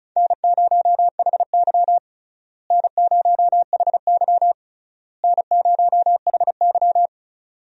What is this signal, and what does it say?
Morse code, heard as a keyed tone, about 36 wpm, 700 Hz. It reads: N0HY N0HY N0HY
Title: USA Call Signs - 4 Characters - Repeated 3x - 35wpm